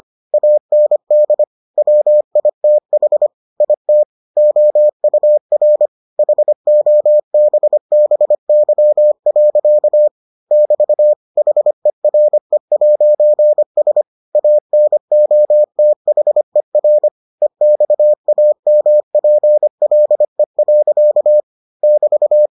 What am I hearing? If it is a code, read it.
Morse code, heard as a keyed tone, about 25 wpm, 600 Hz. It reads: AND WITH IT OUR HOBBY. <BT> HERE'S ANOTHER EXAMPLE. <BT>